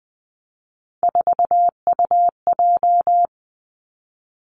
Morse code, heard as a keyed tone, about 20 wpm, 700 Hz. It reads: 4UJ
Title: Three Challenging Letter-Number Combinations 20wpm